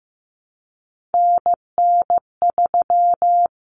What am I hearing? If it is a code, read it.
Morse code, heard as a keyed tone, about 15 wpm, 700 Hz. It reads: NN3
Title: Common Cut Numbers - 3-Digit 15wpm